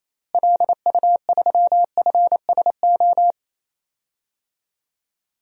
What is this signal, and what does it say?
Morse code, heard as a keyed tone, about 28 wpm, 700 Hz. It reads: LU3FSO
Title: DXCC-100 Call Signs - 6 Characters - Repeated 2x - 28wpm